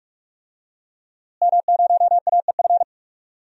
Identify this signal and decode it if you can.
Morse code, heard as a keyed tone, about 41 wpm, 700 Hz. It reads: M0AEF
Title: DXCC-100 Call Signs - 5 Characters - Repeated 2x - 45wpm